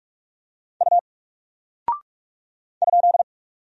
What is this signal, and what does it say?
Morse code, heard as a keyed tone, about 46 wpm, 700 Hz. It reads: U ?
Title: Single Character (Letter:Number:Punctuation:Prosign) Alt 45wpm